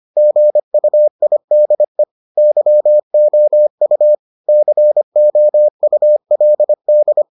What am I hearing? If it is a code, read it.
Morse code, heard as a keyed tone, about 25 wpm, 600 Hz. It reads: GUIDE YOU COULD